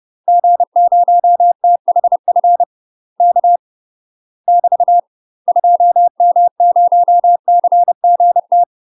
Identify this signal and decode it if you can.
Morse code, heard as a keyed tone, about 30 wpm, 700 Hz. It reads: G0THF K <BT> 2M0CGT
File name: UK MASTER QSOs TRAINING / 30WPM / UK MASTER QSOs 0031